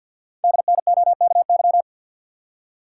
Morse code, heard as a keyed tone, about 54 wpm, 690 Hz. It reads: DNOKY
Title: Code Groups - 5 Letters - 50wpm